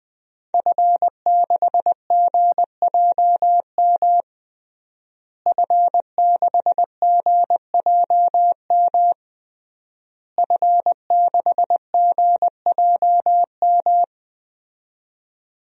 Morse code, heard as a keyed tone, about 20 wpm, 700 Hz. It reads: F6GJM F6GJM F6GJM
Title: DXCC-100 Call Signs - 5 Characters - Repeated 3x - 20wpm